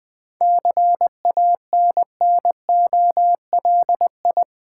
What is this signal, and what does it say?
Morse code, heard as a keyed tone, about 21 wpm, 700 Hz. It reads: CANNOLI
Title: Difficult Words with Hints - 20wpm